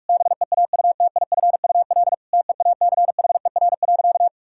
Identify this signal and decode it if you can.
Morse code, heard as a keyed tone, about 46 wpm, 700 Hz. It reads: BEAUTIFUL TEACHER.
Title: Sentences from Top 1000 Words - Review Part 6 - 45wpm